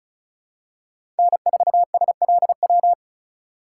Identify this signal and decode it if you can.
Morse code, heard as a keyed tone, about 36 wpm, 700 Hz. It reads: N4SLW
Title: USA Call Signs - 5 Characters - 4x Character Spacing - 35wpm